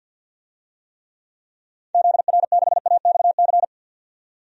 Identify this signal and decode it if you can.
Morse code, heard as a keyed tone, about 50 wpm, 700 Hz. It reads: ZR6AXC
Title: DXCC-100 Call Signs - 6 Characters - 50wpm